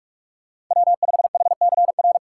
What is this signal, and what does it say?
Morse code, heard as a keyed tone, about 45 wpm, 700 Hz. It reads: W5HCR
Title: USA Call Signs - 5 Characters - 3x Character Spacing - 45wpm